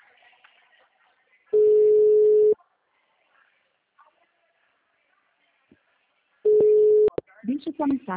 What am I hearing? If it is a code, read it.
Ringback tone (Europe, 425 Hz), with speech.